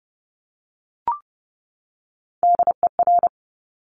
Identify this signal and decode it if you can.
Morse code, heard as a keyed tone, about 30 wpm, 700 Hz. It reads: DER
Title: Top Three Letter Combinations in Words 30wpm